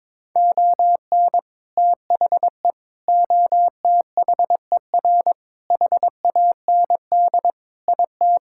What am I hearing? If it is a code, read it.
Morse code, heard as a keyed tone, about 22 wpm, 700 Hz. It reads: ON THE OTHER HAND IT
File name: practice / 22wpm / Sentences with 3-Word Phrases - 22wpm